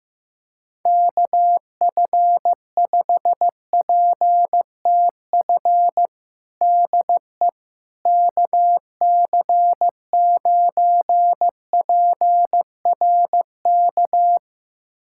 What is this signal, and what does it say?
Morse code, heard as a keyed tone, about 15 wpm, 700 Hz. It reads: KF5PTF DE KC9PRK
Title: USA Callsign DE Callsign - 6 Characters - 15wpm